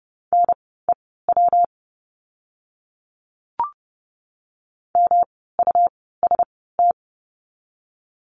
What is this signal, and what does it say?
Morse code, heard as a keyed tone, about 30 wpm, spaced as for 17 wpm, 700 Hz. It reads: NEW MUST
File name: course / 30:10wpm / 237 Lesson 38f - Letter q - Words 30wpm